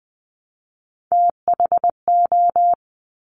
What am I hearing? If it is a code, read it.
Morse code, heard as a keyed tone, about 20 wpm, 700 Hz. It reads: THO
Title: Top Three Letter Combinations in Words 20wpm